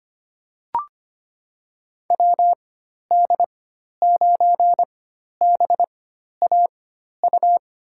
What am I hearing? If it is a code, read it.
Morse code, heard as a keyed tone, about 25 wpm, spaced as for 12 wpm, 700 Hz. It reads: WD9BAU
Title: USA Call Signs - 6 Characters - 4x Character Spacing - 25wpm